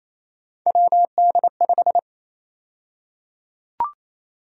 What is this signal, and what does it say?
Morse code, heard as a keyed tone, about 28 wpm, 700 Hz. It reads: WD5
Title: USA Call Sign Prefixes - Repeated 2x - 28wpm